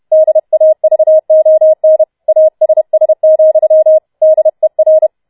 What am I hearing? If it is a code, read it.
Morse code, heard as a keyed tone, about 31 wpm, 630 Hz. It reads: DAVON ASS, DER